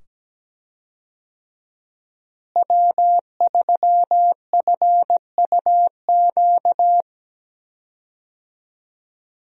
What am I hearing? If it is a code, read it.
Morse code, heard as a keyed tone, about 17 wpm, 700 Hz. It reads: W3FUQ
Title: USA Call Signs - 5 Characters - Repeated 2x - 17wpm